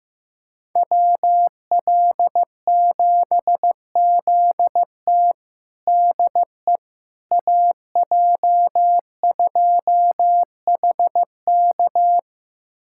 Morse code, heard as a keyed tone, about 16 wpm, 700 Hz. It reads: WL7ZT DE AJ2HK
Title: USA Callsign DE Callsign - 5 Characters - 15wpm